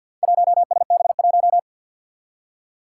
Morse code, heard as a keyed tone, about 47 wpm, 700 Hz. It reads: 1SB1
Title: Four Challenging Letter-Number Combinations 50wpm